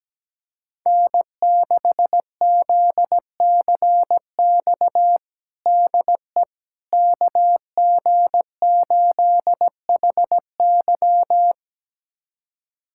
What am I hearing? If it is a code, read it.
Morse code, heard as a keyed tone, about 17 wpm, 700 Hz. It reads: N6ZCX DE KG8HY
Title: USA Callsign DE Callsign 17wpm